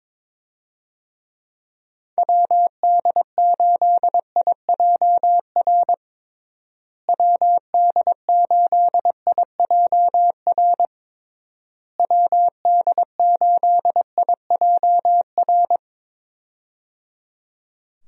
Morse code, heard as a keyed tone, about 21 wpm, 700 Hz. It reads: WD8IJR WD8IJR WD8IJR
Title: USA Call Signs - 6 Characters - Repeated 3x - 22wpm